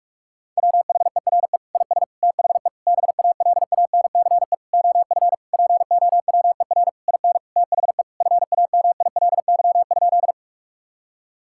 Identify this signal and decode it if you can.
Morse code, heard as a keyed tone, about 41 wpm, 700 Hz. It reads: WHERE IS THE BALANCE OF POWER IN THE FAMILY?
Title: Sentences from Top 1000 Words - Review Part 6 - 45wpm